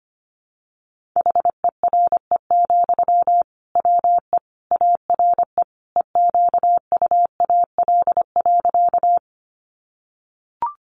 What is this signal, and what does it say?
Morse code, heard as a keyed tone, about 25 wpm, 700 Hz. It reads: HERE, WE ARE EQUAL.
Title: Sentences from Top 1400 Words 25wpm